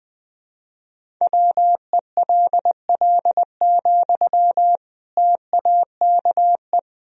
Morse code, heard as a keyed tone, about 21 wpm, 700 Hz. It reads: WELL, TAKE
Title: Sentences from Top 1000 Words - Review Part 1 - 20wpm